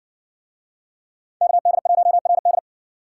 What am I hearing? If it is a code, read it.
Morse code, heard as a keyed tone, about 60 wpm, 700 Hz. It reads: KD1RD